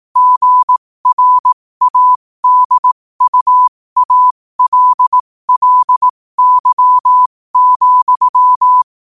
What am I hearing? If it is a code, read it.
Morse code, heard as a keyed tone, about 18 wpm, spaced as for 15 wpm, 1000 Hz. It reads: GRADUALLY,
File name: qotdfiles / 15wpm